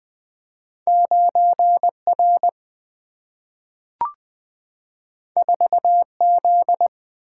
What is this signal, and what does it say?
Morse code, heard as a keyed tone, about 20 wpm, 700 Hz. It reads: 9 R 4 Z
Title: Single-Letter Single-Number 20wpm